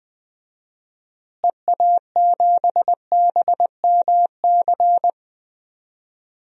Morse code, heard as a keyed tone, about 20 wpm, 700 Hz. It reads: EA7BMC
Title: DXCC-100 Call Signs - 6 Characters - Repeated 2x - 20wpm